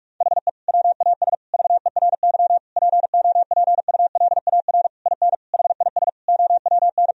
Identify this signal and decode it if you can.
Morse code, heard as a keyed tone, about 46 wpm, 700 Hz. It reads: HE WAS VERY POPULAR IN HIS OWN
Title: Sentences from Top 1000 Words - Review Part 3 - 45wpm